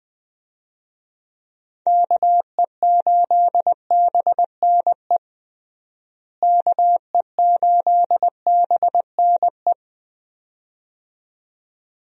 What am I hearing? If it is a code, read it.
Morse code, heard as a keyed tone, about 20 wpm, 700 Hz. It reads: KE8BNE KE8BNE
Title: USA Call Signs - 6 Characters - Repeated 2x - 20wpm